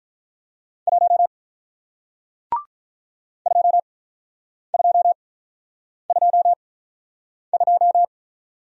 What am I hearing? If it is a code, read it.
Morse code, heard as a keyed tone, about 48 wpm, spaced as for 9 wpm, 700 Hz. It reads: 1 2222